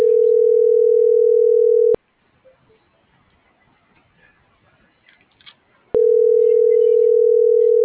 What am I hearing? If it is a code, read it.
Ringback tone (North America), 440 + 480 Hz.